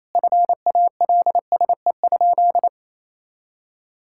Morse code, heard as a keyed tone, about 28 wpm, 700 Hz. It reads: FALSE?
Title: Sentences from Top 1600 Words 28wpm